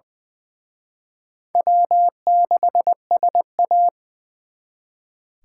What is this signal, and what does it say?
Morse code, heard as a keyed tone, about 24 wpm, 700 Hz. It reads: W6SA
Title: Speed Racing - USA 1x2 Call Signs 20wpm